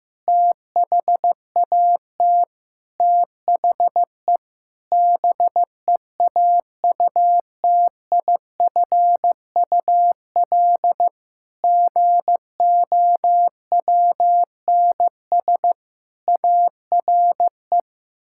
Morse code, heard as a keyed tone, about 15 wpm, 700 Hz. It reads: THAT THE BEAUTIFUL GOWNS ARE
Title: Pangram Challenge v2 - 15wpm